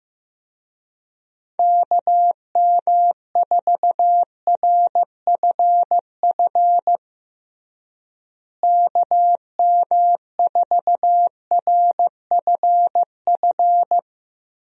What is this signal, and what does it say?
Morse code, heard as a keyed tone, about 16 wpm, 700 Hz. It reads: KM4RFF KM4RFF
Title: USA Call Signs - 6 Characters - Repeated 3x - 15wpm